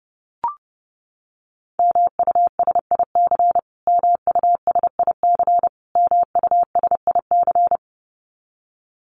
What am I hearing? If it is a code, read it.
Morse code, heard as a keyed tone, about 30 wpm, 700 Hz. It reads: MUSIC MUSIC MUSIC
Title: Top 500 Words Repeated 30wpm